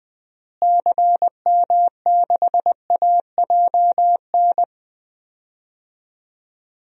Morse code, heard as a keyed tone, about 21 wpm, 700 Hz. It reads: CM6AJN
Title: DXCC-100 Call Signs - 6 Characters - Repeated 3x - 20wpm